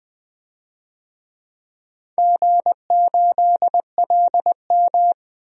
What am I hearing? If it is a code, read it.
Morse code, heard as a keyed tone, about 20 wpm, 700 Hz. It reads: G8LM
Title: DXCC-100 Call Signs - 4 Characters - 20wpm